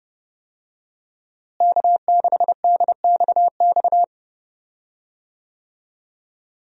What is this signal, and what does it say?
Morse code, heard as a keyed tone, about 30 wpm, 700 Hz. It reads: K6DXX
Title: USA Call Signs - 5 Characters - Repeated 3x - 30wpm